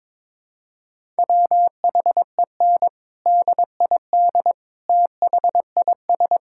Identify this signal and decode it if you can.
Morse code, heard as a keyed tone, about 22 wpm, 700 Hz. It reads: WHEN DID THIS
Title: Sentences from Top 1000 Words - Review Part 6 - 22wpm